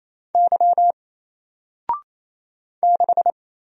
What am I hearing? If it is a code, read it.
Morse code, heard as a keyed tone, about 27 wpm, 700 Hz. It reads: Y 6
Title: Single Letter-Number 28wpm